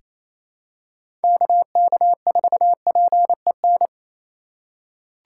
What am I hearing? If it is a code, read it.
Morse code, heard as a keyed tone, about 28 wpm, 700 Hz. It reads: KK4PEN
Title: USA Call Signs - 6 Characters - 2x Character Spacing - 28wpm